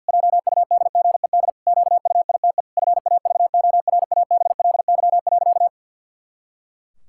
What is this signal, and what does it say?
Morse code, heard as a keyed tone, about 50 wpm, 700 Hz. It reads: JUDGED QUITE FAVORABLY.
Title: Pangram Challenge v2 - 50wpm